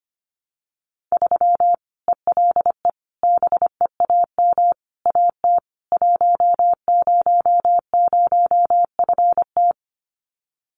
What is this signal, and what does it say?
Morse code, heard as a keyed tone, about 25 wpm, 700 Hz. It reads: 3 ELE BEAM AT 100FT
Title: Common QSO Phrases 25wpm